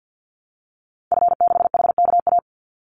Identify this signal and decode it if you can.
Morse code, heard as a keyed tone, about 48 wpm, 700 Hz. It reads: F6HKA
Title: Call Signs Repeated 2x - 50wpm